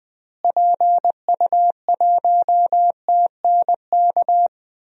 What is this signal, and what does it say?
Morse code, heard as a keyed tone, about 20 wpm, 700 Hz. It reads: PU1TNK
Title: DXCC-100 Call Signs - 6 Characters - Repeated 3x - 20wpm